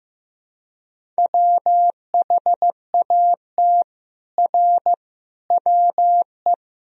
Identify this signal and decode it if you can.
Morse code, heard as a keyed tone, about 15 wpm, 700 Hz. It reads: WHAT R WE
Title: Sentences from Top 100 Words and Abbreviations 15wpm